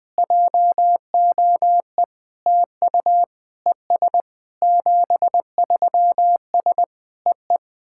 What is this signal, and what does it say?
Morse code, heard as a keyed tone, about 21 wpm, 700 Hz. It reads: JOE TU ES 73S EE
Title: SOTA QSO Practice 20wpm